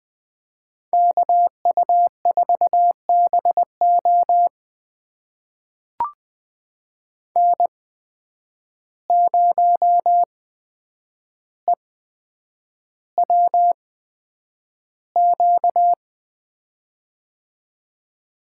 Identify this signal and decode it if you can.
Morse code, heard as a keyed tone, about 20 wpm, spaced as for 9 wpm, 700 Hz. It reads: KU4BO N0EWQ
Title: USA Call Signs - 5 Characters - 8x Character Spacing - 20wpm